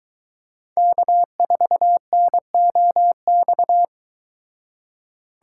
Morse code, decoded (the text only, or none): K4NOX